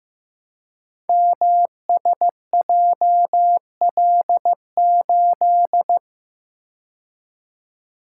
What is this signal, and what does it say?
Morse code, heard as a keyed tone, about 15 wpm, 700 Hz. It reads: MSJL8
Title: Code Groups - 5 Letter:Numbers - 15wpm